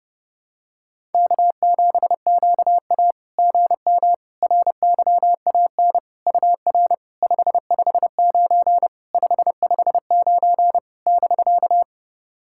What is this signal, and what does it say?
Morse code, heard as a keyed tone, about 30 wpm, 690 Hz. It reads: K7QA GM RYAN UR 559 559 <BK>
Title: SOTA QSO Practice 30wpm